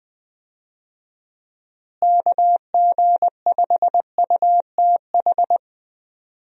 Morse code, heard as a keyed tone, about 20 wpm, 700 Hz. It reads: KG5UTH
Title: USA Call Signs - 6 Characters - Repeated 3x - 20wpm